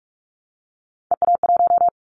Morse code, heard as a keyed tone, about 44 wpm, 700 Hz. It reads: EA1
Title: Callsign Prefixes for Top 25 DXCC - Repeated 3x - 45wpm